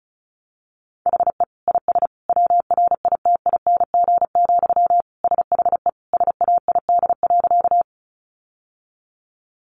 Morse code, heard as a keyed tone, about 34 wpm, 700 Hz. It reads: HE IS WRITING, SHE SAID.
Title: Sentences from Top 800 Words 35wpm